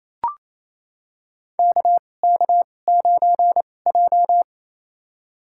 Morse code, heard as a keyed tone, about 28 wpm, spaced as for 20 wpm, 700 Hz. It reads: KK9J
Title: USA Call Signs - 4 Characters - 2x Character Spacing - 28wpm